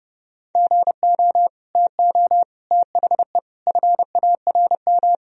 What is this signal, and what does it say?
Morse code, heard as a keyed tone, about 31 wpm, 700 Hz. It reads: GO TO THE FARM
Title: Sentences from Top 1000 Words - Review Part 2 - 30wpm